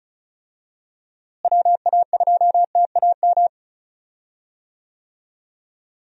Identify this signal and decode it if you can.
Morse code, heard as a keyed tone, about 34 wpm, 700 Hz. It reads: WA2TAM
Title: USA Call Signs - 6 Characters - Repeated 3x - 35wpm